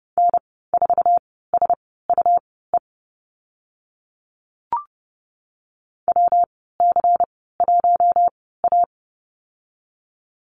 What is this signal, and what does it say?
Morse code, heard as a keyed tone, about 30 wpm, spaced as for 17 wpm, 700 Hz. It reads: N4SUE WC1A
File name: Lesson 19g - Letter w - Call Signs 30wpm